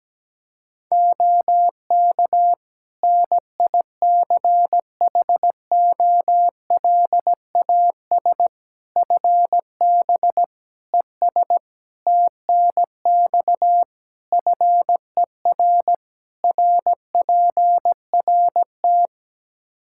Morse code, heard as a keyed tone, about 17 wpm, 700 Hz. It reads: OK NICHOLAS FB ES TNX FER RPRT
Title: Practice QSO - Exchange 2 - Info - 17wpm